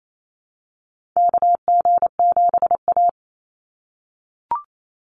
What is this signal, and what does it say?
Morse code, heard as a keyed tone, about 28 wpm, 700 Hz. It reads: KG7A
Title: Call Signs - 28wpm